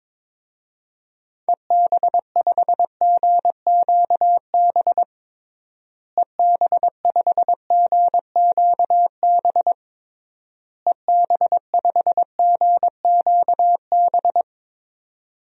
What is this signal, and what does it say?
Morse code, heard as a keyed tone, about 22 wpm, 700 Hz. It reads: EB5GQB EB5GQB EB5GQB
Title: DXCC-100 Call Signs - 6 Characters - Repeated 3x - 22wpm